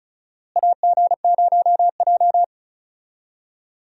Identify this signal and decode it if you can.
Morse code, heard as a keyed tone, about 36 wpm, 700 Hz. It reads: AG0J
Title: USA Call Signs - 4 Characters - Repeated 2x - 35wpm